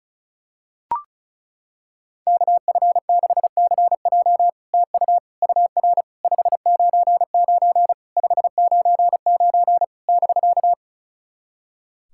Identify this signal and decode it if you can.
Morse code, heard as a keyed tone, about 36 wpm, 700 Hz. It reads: KF6CJ TU UR 599 599 <BK>
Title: SOTA QSO Practice 35wpm